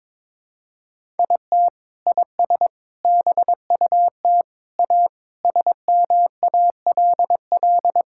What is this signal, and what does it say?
Morse code, heard as a keyed tone, about 22 wpm, 700 Hz. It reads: IT IS BUT A SMALL
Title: Sentences from Top 1000 Words - Review Part 5 - 22wpm